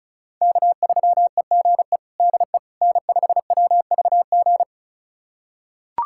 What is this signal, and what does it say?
Morse code, heard as a keyed tone, about 36 wpm, 700 Hz. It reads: K3EGE DE N5WVG
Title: USA Callsign DE Callsign - 5 Characters - 35wpm